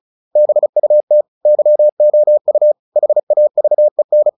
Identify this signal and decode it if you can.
Morse code, heard as a keyed tone, about 34 wpm, 600 Hz. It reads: BUT YOU HAVEN
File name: the old man and the sea / the old man and the sea 1 35wpm